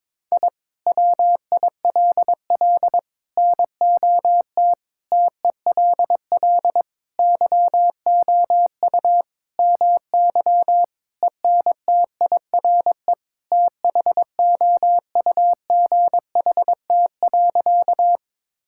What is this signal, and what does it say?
Morse code, heard as a keyed tone, about 22 wpm, 700 Hz. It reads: I WILL NOT TELL YOU MY ENTIRE THOUGHT.
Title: Sentences from Top 1000 Words - Review Part 5 - 22wpm